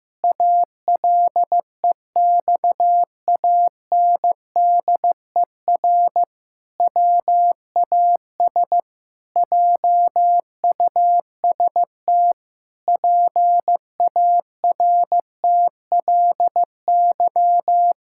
Morse code, heard as a keyed tone, about 15 wpm, 700 Hz. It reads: ALEXANDER WAS JUST PARTLY
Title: Pangram Challenge v2 - 15wpm